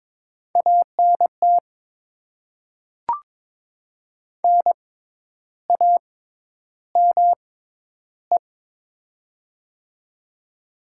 Morse code, heard as a keyed tone, about 22 wpm, spaced as for 10 wpm, 700 Hz. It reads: ANT NAME
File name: Top 20 QSO Elements - 6x Character Spacing - 22wpm